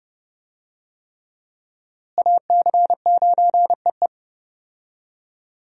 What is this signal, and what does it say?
Morse code, heard as a keyed tone, about 30 wpm, 700 Hz. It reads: AC9EE
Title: USA Call Signs - 5 Characters - Repeated 3x - 30wpm